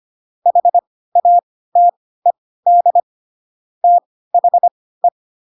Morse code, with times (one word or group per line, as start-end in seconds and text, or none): 0.5-3.0 HATED
3.8-5.1 THE